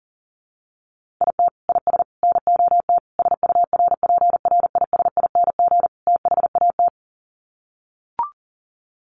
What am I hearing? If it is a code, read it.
Morse code, heard as a keyed tone, about 40 wpm, 700 Hz. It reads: IT IS NOT SURPRISING THAT
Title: Sets of 5 Words 40wpm